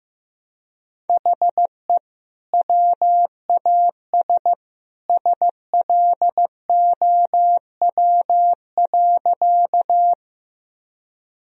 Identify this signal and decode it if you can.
Morse code, heard as a keyed tone, about 15 wpm, 700 Hz. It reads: HE WAS SLOW.
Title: Sentences from Top 1000 Words - Review Part 2 - 15wpm